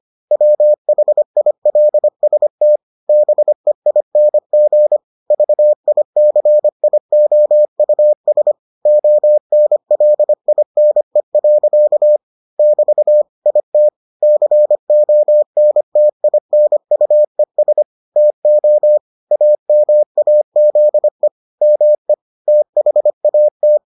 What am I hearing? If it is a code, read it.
Morse code, heard as a keyed tone, about 25 wpm, 600 Hz. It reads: WHILST BEING VICIOUS ONLINE. <BT> IT CONTINUES TO AMAZE ME THAT